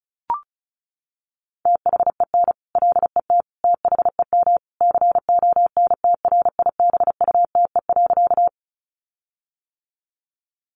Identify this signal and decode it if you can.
Morse code, heard as a keyed tone, about 34 wpm, 700 Hz. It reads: THEN LET THEM CONTRIBUTE.
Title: Sentences from Top 1400 Words 35wpm